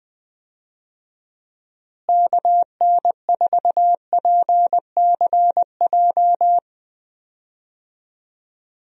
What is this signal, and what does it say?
Morse code, heard as a keyed tone, about 20 wpm, 700 Hz. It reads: KN4PCJ
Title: USA Call Signs - 6 Characters - Repeated 3x - 20wpm